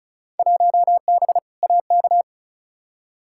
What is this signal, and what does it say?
Morse code, heard as a keyed tone, about 34 wpm, 700 Hz. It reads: 1B AK
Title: Field Day Exchanges Encore - 35wpm